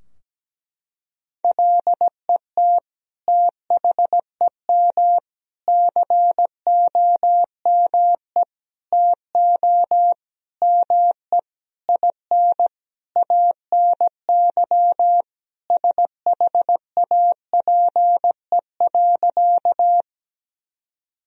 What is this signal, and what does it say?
Morse code, heard as a keyed tone, about 17 wpm, 700 Hz. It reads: LET THEM COME TO ME IN ANY SHAPE.
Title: Sentences from Top 1000 Words - Review Part 3 - 17wpm